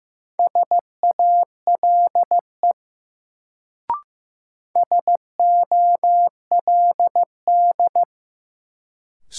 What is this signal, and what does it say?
Morse code, heard as a keyed tone, about 15 wpm, 700 Hz. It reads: SALE SOLD
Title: Sentences from Top 1400 Words - New Word Drill - 15wpm